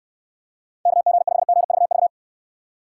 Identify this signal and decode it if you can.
Morse code, heard as a keyed tone, about 64 wpm, 700 Hz. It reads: KG5GVV